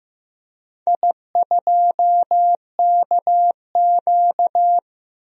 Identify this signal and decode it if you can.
Morse code, heard as a keyed tone, about 16 wpm, 700 Hz. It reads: I2KQ
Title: DXCC-100 Call Signs - 4 Characters - Repeated 3x - 15wpm